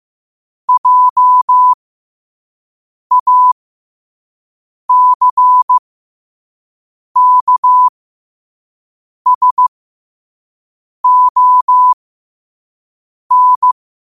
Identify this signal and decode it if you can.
Morse code, heard as a keyed tone, about 15 wpm, spaced as for 5 wpm, 1000 Hz. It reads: JACKSON
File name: qotdfiles / 5wpm